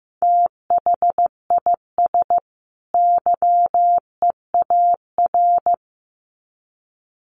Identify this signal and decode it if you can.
Morse code, heard as a keyed tone, about 15 wpm, 700 Hz. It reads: THIS YEAR
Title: Sets of 2 Words 15wpm